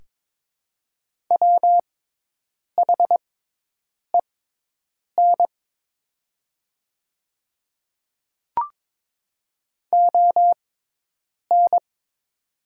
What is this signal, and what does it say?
Morse code, heard as a keyed tone, about 22 wpm, spaced as for 8 wpm, 700 Hz. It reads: WHEN ON
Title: 6x Character Spacing - Top 100 Words - 22wpm